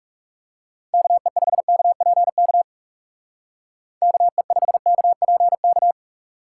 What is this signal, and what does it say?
Morse code, heard as a keyed tone, about 37 wpm, 700 Hz. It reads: KE5KPK KE5KPK
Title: Speed Racing - USA Call Signs 30wpm